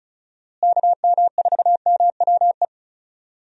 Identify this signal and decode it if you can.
Morse code, heard as a keyed tone, about 36 wpm, 700 Hz. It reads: KM4MWE
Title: USA Call Signs - 6 Characters - Repeated 3x - 35wpm